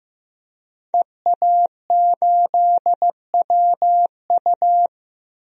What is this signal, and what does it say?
Morse code, heard as a keyed tone, about 15 wpm, 700 Hz. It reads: EA8WU
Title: DXCC-100 Call Signs - 5 Characters - 15wpm